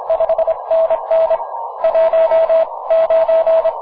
Morse code, heard as a keyed tone, about 30 wpm, spaced as for 24 wpm, 690 Hz. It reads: 5NN 19
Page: На 15 метров, в лучшем виде - прекрасный сигнал!